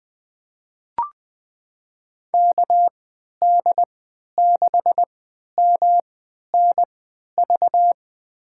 Morse code, heard as a keyed tone, about 20 wpm, spaced as for 11 wpm, 700 Hz. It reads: KD6MNV
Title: USA Call Signs - 6 Characters - 3x Character Spacing - 20wpm